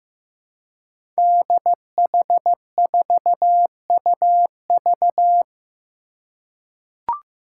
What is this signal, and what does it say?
Morse code, heard as a keyed tone, about 15 wpm, 700 Hz. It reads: DH4UV
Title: Five Challenging Letter-Number Combinations 15wpm